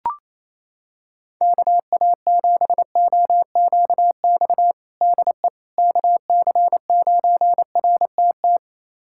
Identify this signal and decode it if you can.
Morse code, heard as a keyed tone, about 28 wpm, 700 Hz. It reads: KA7OQX DE KC9RTT
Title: USA Callsign DE Callsign - 6 Characters - 28wpm